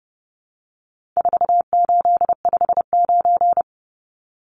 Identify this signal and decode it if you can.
Morse code, heard as a keyed tone, about 30 wpm, 700 Hz. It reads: 4859
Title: Numbers - 4-Digit 30wpm